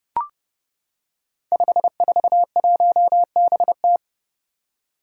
Morse code, heard as a keyed tone, about 30 wpm, 700 Hz. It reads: 541BT
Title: Five Challenging Letter-Number Combinations 30wpm